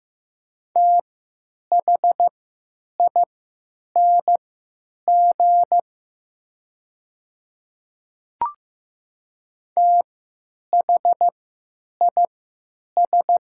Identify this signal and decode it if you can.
Morse code, heard as a keyed tone, about 15 wpm, spaced as for 9 wpm, 700 Hz. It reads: THING THIS
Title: 3x Character Spacing - Top 100 Words - 15wpm